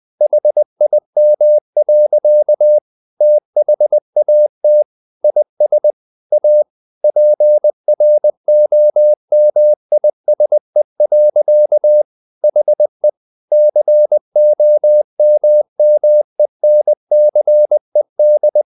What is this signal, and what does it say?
Morse code, heard as a keyed tone, about 20 wpm, 600 Hz. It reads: HIM. THAT IS A PROMISE. HE COMMENCED
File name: the old man and the sea / the old man and the sea part 1 20wpm